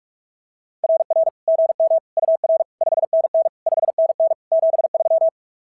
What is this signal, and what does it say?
Morse code, heard as a keyed tone, about 44 wpm, 690 Hz. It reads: RR GM UR 5NN 5NN 73
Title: SOTA QSO Practice 45wpm